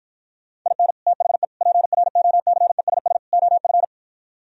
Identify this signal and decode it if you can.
Morse code, heard as a keyed tone, about 52 wpm, 700 Hz. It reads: IN THE PROCESS OF